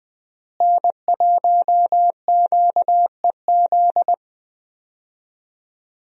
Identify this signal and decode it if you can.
Morse code, heard as a keyed tone, about 20 wpm, 700 Hz. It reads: N1QEZ
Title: USA Call Signs - 5 Characters - Repeated 3x - 20wpm